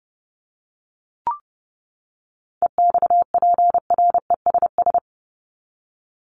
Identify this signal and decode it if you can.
Morse code, heard as a keyed tone, about 30 wpm, 700 Hz. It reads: EXPRESS
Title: Top 1001-1250 Words 30wpm